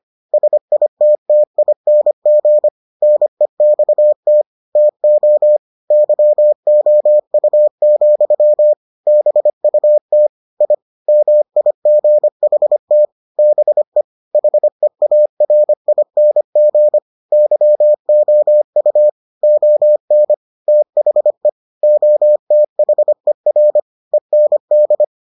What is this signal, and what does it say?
Morse code, heard as a keyed tone, about 25 wpm, 600 Hz. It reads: SITTING NEXT TO YOU, BUT I MIGHT BE HEARING YOU ON THE OTHER END